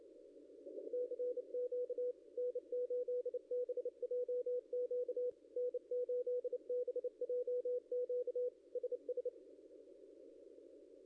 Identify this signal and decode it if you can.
Morse code, heard as a keyed tone, about 30 wpm, 490 Hz. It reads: CQ N8BJQ N8BJQ SS
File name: contests / ss-cw-k5tr-op-n5rz -0600